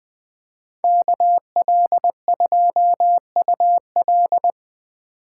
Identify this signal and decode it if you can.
Morse code, heard as a keyed tone, about 21 wpm, 700 Hz. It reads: KL2UL